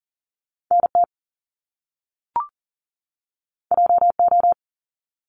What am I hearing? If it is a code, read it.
Morse code, heard as a keyed tone, about 38 wpm, 700 Hz. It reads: NT JO
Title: Top Two Letter Combinations in Words 40wpm